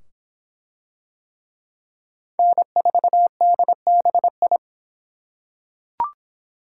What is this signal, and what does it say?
Morse code, heard as a keyed tone, about 24 wpm, 700 Hz. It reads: N4DBI